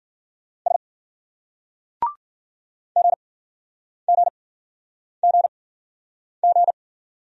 Morse code, heard as a keyed tone, about 50 wpm, spaced as for 9 wpm, 700 Hz. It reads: S GGGG